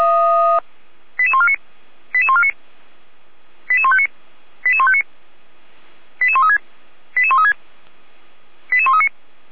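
Five-tone selective call (ZVEI), heard as quick runs of five tones, tones of about 70 ms. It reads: 88149 88149 88159 88159 88136 88136 88139
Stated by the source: ZVEI five-tone call received on an SDR, from fire brigade feuerwehr-orlamuende.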